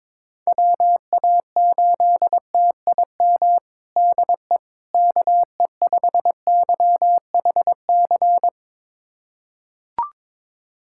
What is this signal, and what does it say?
Morse code, heard as a keyed tone, about 22 wpm, 700 Hz. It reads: WA8TIM DE KE5YHC
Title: USA Callsign DE Callsign 22wpm